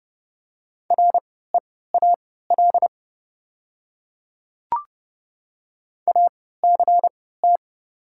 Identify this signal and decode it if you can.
Morse code, heard as a keyed tone, about 29 wpm, spaced as for 17 wpm, 700 Hz. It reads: REAL ACT
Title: 3x Character Spacing - Top 300 Words - 30wpm